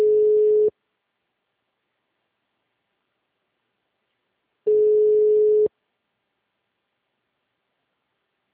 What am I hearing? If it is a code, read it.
Ringback tone (Europe, 425 Hz).